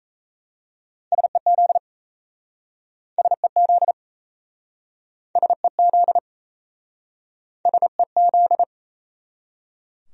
Morse code, heard as a keyed tone, about 38 wpm, 700 Hz. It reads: SEZ SEZ SEZ SEZ